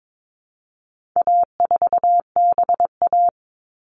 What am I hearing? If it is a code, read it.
Morse code, heard as a keyed tone, about 22 wpm, 700 Hz. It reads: A4BA
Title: All Cut Numbers - 4-Digit 22wpm